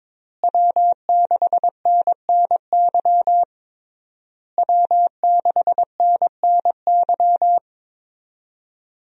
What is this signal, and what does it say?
Morse code, heard as a keyed tone, about 21 wpm, 700 Hz. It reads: W6NNY W6NNY
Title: USA Call Signs - 5 Characters - Repeated 3x - 22wpm